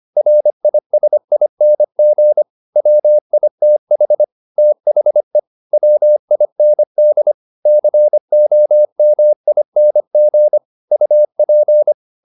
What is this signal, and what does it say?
Morse code, heard as a keyed tone, about 25 wpm, 600 Hz. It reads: RISING WITH THE WIND COMING UP
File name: the old man and the sea part 1 25wpm